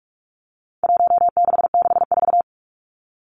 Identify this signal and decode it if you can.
Morse code, heard as a keyed tone, about 44 wpm, 700 Hz. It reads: 1664
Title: Numbers - Contest Style - 4-Digit 45wpm